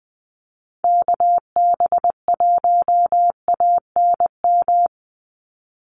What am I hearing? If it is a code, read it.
Morse code, heard as a keyed tone, about 20 wpm, 700 Hz. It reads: KB1ANM
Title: USA Call Signs 20wpm